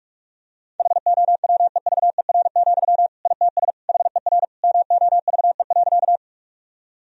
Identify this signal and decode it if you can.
Morse code, heard as a keyed tone, about 46 wpm, 700 Hz. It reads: HOWEVER, ITS HER MOVE.
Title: Sentences from Top 1000 Words - Review Part 1 - 45wpm